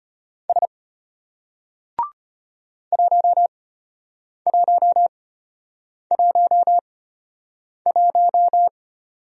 Morse code, heard as a keyed tone, about 33 wpm, spaced as for 8 wpm, 700 Hz. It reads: S 1111